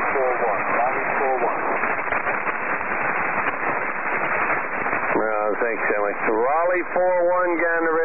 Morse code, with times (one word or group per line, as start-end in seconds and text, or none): none